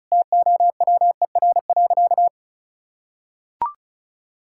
Morse code, heard as a keyed tone, about 36 wpm, 700 Hz. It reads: TOWER.
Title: Pangram Challenge v2 - 35wpm